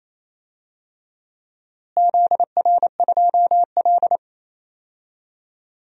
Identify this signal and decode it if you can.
Morse code, heard as a keyed tone, about 28 wpm, 700 Hz. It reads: ZR2L
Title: DXCC-100 Call Signs - 4 Characters - Repeated 3x - 28wpm